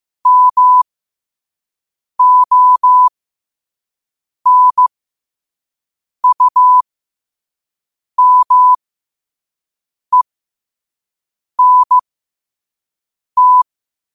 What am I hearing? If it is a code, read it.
Morse code, heard as a keyed tone, about 15 wpm, spaced as for 5 wpm, 1000 Hz. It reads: MONUMENT
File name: 5wpm